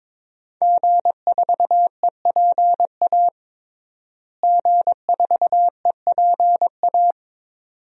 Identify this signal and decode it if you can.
Morse code, heard as a keyed tone, about 22 wpm, 700 Hz. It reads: G4EPA G4EPA
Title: DXCC-100 Call Signs - 5 Characters - Repeated 2x - 22wpm